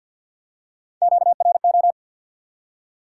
Morse code, heard as a keyed tone, about 53 wpm, 700 Hz. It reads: QRO
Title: Top 90 QSO Elements 50wpm